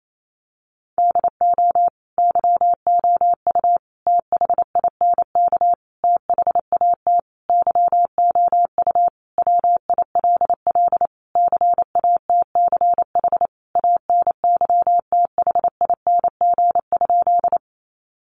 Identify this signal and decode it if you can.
Morse code, heard as a keyed tone, about 28 wpm, 700 Hz. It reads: DO YOU THINK THAT YOU WILL CATCH ANYTHING?
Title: Sentences from Top 1500 Words 28wpm